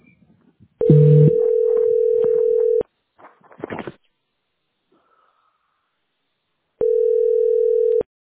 Ringback tone (North America), 440 + 480 Hz.